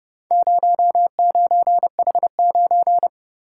Morse code, heard as a keyed tone, about 30 wpm, 700 Hz. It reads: 09H9